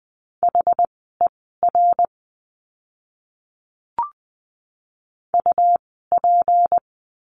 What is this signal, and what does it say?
Morse code, heard as a keyed tone, about 20 wpm, spaced as for 14 wpm, 700 Hz. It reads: HER UP
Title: 199 Lesson 32f - Letter k - Words 20wpm